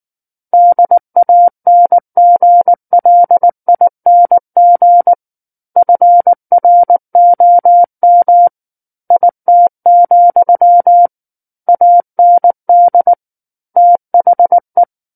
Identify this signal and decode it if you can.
Morse code, heard as a keyed tone, about 19 wpm, 700 Hz. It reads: DANGLING FROM IT, AND THE
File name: Book One / 10-Chapter 10 - 19 WPM